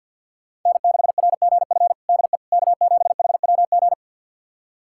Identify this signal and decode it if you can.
Morse code, heard as a keyed tone, about 50 wpm, 700 Hz. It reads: N6RGF DE K7HWG
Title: USA Callsign DE Callsign - 5 Characters - 50wpm